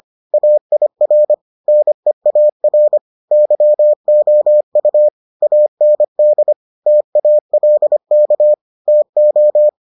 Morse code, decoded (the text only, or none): AIR NEAR YOU AND TALK TO